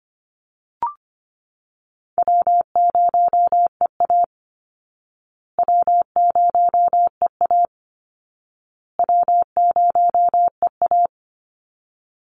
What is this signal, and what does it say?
Morse code, heard as a keyed tone, about 25 wpm, 700 Hz. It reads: W0EA W0EA W0EA
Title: Call Signs Repeated 2x - 25wpm